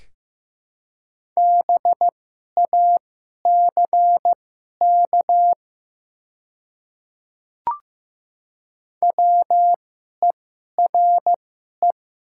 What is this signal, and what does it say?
Morse code, heard as a keyed tone, about 15 wpm, spaced as for 11 wpm, 700 Hz. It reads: BACK WERE
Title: 2x Character Spacing - Top 200 Words - 15wpm